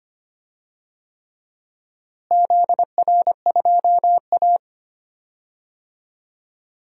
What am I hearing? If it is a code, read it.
Morse code, heard as a keyed tone, about 25 wpm, 700 Hz. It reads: ZR2A
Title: DXCC-100 Call Signs - 4 Characters - Repeated 3x - 25wpm